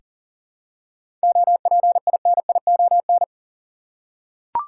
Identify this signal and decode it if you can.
Morse code, heard as a keyed tone, about 40 wpm, 700 Hz. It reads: OPINION
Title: Seven Letter Words - 40wpm